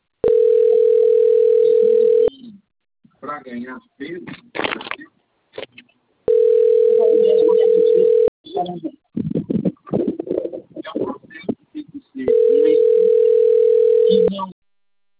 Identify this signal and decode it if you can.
Ringback tone (North America), 440 + 480 Hz, with speech.
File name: rg-1600-unknown- - - .47549